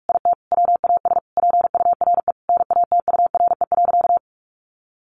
Morse code, heard as a keyed tone, about 44 wpm, 700 Hz. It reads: IT WAS PURE NATURE.
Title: Sentences from Top 1500 Words 45wpm